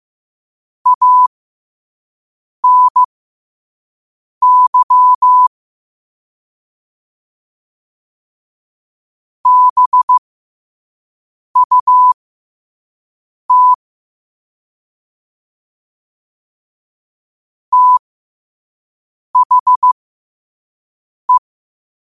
Morse code, heard as a keyed tone, about 15 wpm, spaced as for 5 wpm, 1000 Hz. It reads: ANY BUT THE